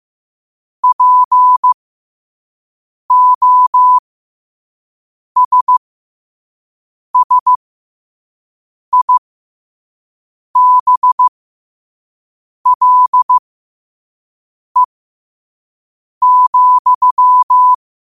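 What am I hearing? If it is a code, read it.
Morse code, heard as a keyed tone, about 15 wpm, spaced as for 5 wpm, 1000 Hz. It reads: POSSIBLE,
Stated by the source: Morse at 5 wpm.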